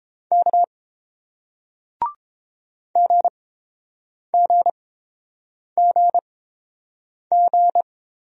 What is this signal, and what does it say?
Morse code, heard as a keyed tone, about 30 wpm, spaced as for 7 wpm, 700 Hz. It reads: K GGGG